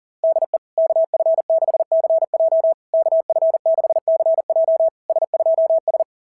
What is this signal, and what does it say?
Morse code, heard as a keyed tone, about 40 wpm, 660 Hz. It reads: DE KF6CJ KF6CJ S2S
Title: SOTA QSO Practice 40wpm